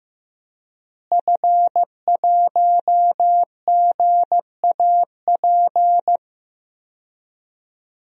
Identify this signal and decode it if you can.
Morse code, heard as a keyed tone, about 15 wpm, 700 Hz. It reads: F1GAP
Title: DXCC-100 Call Signs - 5 Characters - Repeated 3x - 15wpm